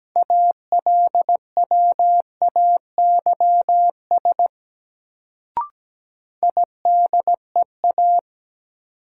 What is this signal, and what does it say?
Morse code, heard as a keyed tone, about 17 wpm, 700 Hz. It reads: ALWAYS IDEA
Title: Sentences from Top 300 Words - New Word Drill - 17wpm